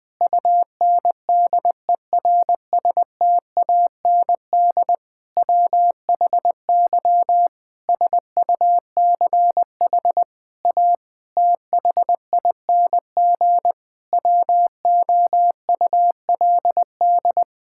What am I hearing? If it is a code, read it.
Morse code, heard as a keyed tone, about 20 wpm, 700 Hz. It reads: UNDERSTAND WHY SUCH A THING WOULD
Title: Sentences from Top 1000 Words - Review Part 6 - 20wpm